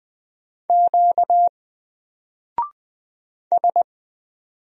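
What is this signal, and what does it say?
Morse code, heard as a keyed tone, about 20 wpm, 700 Hz. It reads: Q S
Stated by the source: Single letter and number Morse drill at 20 wpm, from N4LCD.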